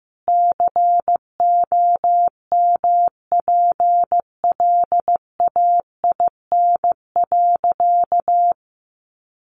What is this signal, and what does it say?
Morse code, heard as a keyed tone, about 15 wpm, 700 Hz. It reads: COMPLAIN.
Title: Sentences from Top 1300 Words 15wpm